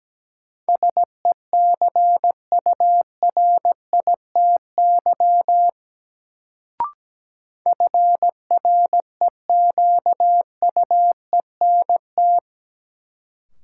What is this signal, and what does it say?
Morse code, heard as a keyed tone, about 17 wpm, 700 Hz. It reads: SECURITY FREQUENT
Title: Sentences from Top 1400 Words - New Word Drill Alt - 17wpm